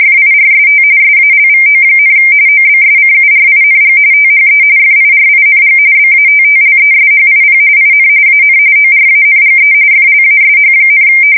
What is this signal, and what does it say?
RTTY, about 45.45 baud. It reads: FIRST SALVO, OBSCURING THE FIRING BATTLESHIP IN AN ORANGE FLASH,
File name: fsk170 / USS-New-Jersey